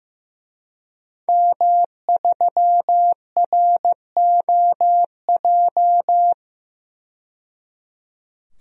Morse code, heard as a keyed tone, about 16 wpm, 700 Hz. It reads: M3ROJ
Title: DXCC-100 Call Signs - 5 Characters - 15wpm